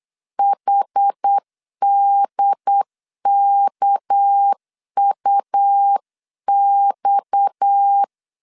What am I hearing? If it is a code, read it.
Morse code, heard as a keyed tone, about 8 wpm, 800 Hz. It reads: HDKUX
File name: 8wpm